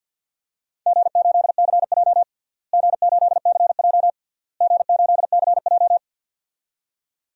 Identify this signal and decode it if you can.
Morse code, heard as a keyed tone, about 49 wpm, 700 Hz. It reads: G8CJ G8CJ G8CJ
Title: DXCC-100 Call Signs - 4 Characters - Repeated 3x - 50wpm